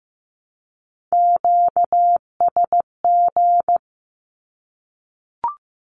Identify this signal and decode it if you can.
Morse code, heard as a keyed tone, about 15 wpm, 700 Hz. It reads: QSG
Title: Q-Codes 15wpm